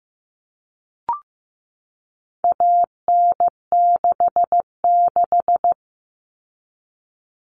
Morse code, heard as a keyed tone, about 15 wpm, 700 Hz. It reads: AN66
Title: All Cut Numbers - 4-Digit 15wpm